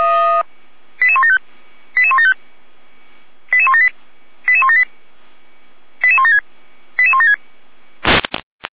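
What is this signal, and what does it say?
Five-tone selective call (ZVEI), heard as quick runs of five tones, tones of about 70 ms. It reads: 88175 88175 88178 88178 88176 88176
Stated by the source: ZVEI five-tone selective call received on SDR.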